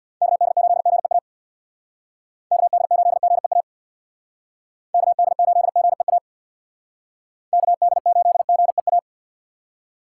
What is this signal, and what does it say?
Morse code, heard as a keyed tone, about 63 wpm, 700 Hz. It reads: KD8GEA KD8GEA KD8GEA KD8GEA